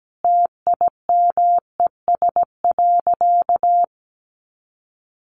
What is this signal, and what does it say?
Morse code, heard as a keyed tone, about 17 wpm, 700 Hz. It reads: TIMES.
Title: Sentences from Top 1100 Words 17wpm